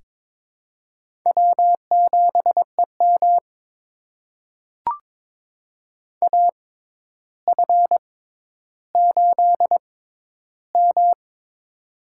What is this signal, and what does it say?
Morse code, heard as a keyed tone, about 22 wpm, spaced as for 11 wpm, 700 Hz. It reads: W7EM AF8M
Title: USA Call Signs - 4 Characters - 6x Character Spacing - 22wpm